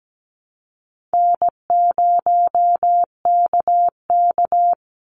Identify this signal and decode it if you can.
Morse code, heard as a keyed tone, about 17 wpm, 700 Hz. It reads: N0KK
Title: Easy Call Signs Repeated 3x - 17wpm